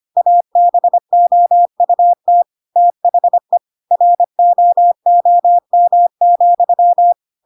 Morse code, heard as a keyed tone, about 25 wpm, 700 Hz. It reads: ABOUT THE ROOM,
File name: Pinocchio / Pinocchio ch5 25 25 ik1hge